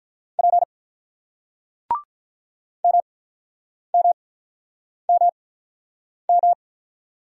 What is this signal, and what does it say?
Morse code, heard as a keyed tone, about 48 wpm, spaced as for 9 wpm, 700 Hz. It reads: P MMMM